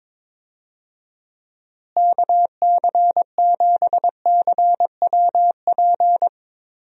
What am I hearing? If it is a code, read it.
Morse code, heard as a keyed tone, about 21 wpm, 700 Hz. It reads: KC7CWP
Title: USA Call Signs - 6 Characters - Repeated 3x - 22wpm